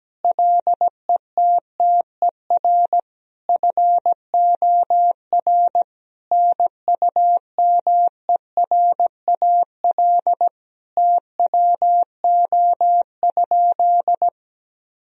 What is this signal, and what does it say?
Morse code, heard as a keyed tone, about 17 wpm, 700 Hz. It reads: LETTER FOR NUMERAL TWO?
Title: Sentences from Top 1000 Words - Review Part 2 - 17wpm